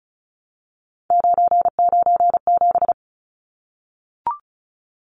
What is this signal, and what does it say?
Morse code, heard as a keyed tone, about 34 wpm, 700 Hz. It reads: 997
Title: Numbers - 3-Digit 35wpm